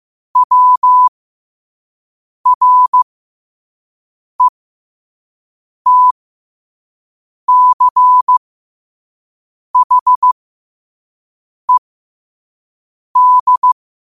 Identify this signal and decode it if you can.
Morse code, heard as a keyed tone, about 15 wpm, spaced as for 5 wpm, 1000 Hz. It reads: WRETCHED
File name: qotdfiles / 5wpm